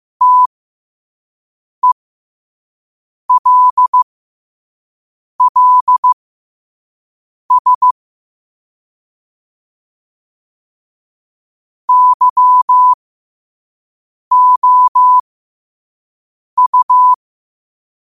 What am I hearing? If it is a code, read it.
Morse code, heard as a keyed tone, about 15 wpm, spaced as for 5 wpm, 1000 Hz. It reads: TELLS YOU
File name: qotdfiles / 5wpm